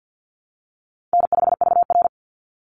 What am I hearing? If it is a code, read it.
Morse code, heard as a keyed tone, about 50 wpm, 700 Hz. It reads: N5VR
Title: Easy Call Signs Repeated 3x - 50wpm